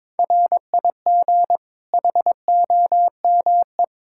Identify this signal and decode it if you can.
Morse code, heard as a keyed tone, about 21 wpm, 700 Hz. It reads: RIG HOME
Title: Practice QSO - Exchange 2 - Info - 22wpm